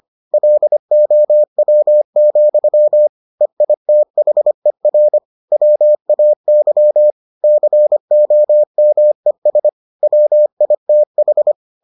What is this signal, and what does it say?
Morse code, heard as a keyed tone, about 25 wpm, 600 Hz. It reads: LOW, EITHER WAY COMES WITH